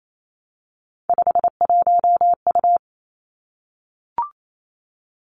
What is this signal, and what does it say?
Morse code, heard as a keyed tone, about 28 wpm, 700 Hz. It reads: 51U
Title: Three Challenging Letter-Number Combinations 28wpm